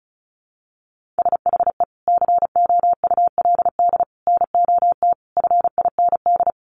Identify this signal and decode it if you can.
Morse code, heard as a keyed tone, about 34 wpm, 700 Hz. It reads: SHE COULD NOT FIND
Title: Sentences from Top 1100 Words 35wpm